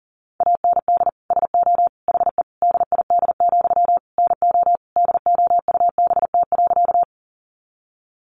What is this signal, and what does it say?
Morse code, heard as a keyed tone, about 40 wpm, 700 Hz. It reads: AND SO HE DID, NO DOUBT.
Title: Sentences from Top 1500 Words 40wpm